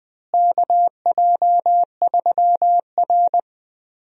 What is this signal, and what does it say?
Morse code, heard as a keyed tone, about 20 wpm, 700 Hz. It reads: KJ3R
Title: USA Call Signs - 4 Characters - Repeated 3x - 20wpm